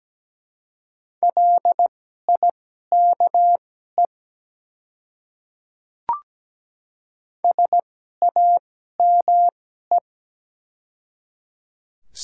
Morse code, heard as a keyed tone, about 17 wpm, spaced as for 12 wpm, 700 Hz. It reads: LIKE SAME
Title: 2x Character Spacing - Top 200 Words - 17wpm